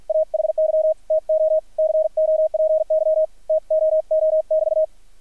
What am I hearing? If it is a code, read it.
Morse code, heard as a keyed tone, about 50 wpm, 630 Hz, with radio noise. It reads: AHY TO COPY TOO<BT>
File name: archived / CRSnet